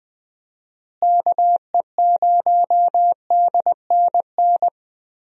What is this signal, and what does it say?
Morse code, heard as a keyed tone, about 20 wpm, 700 Hz. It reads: KE0DNN
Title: USA Call Signs - 6 Characters - Repeated 2x - 20wpm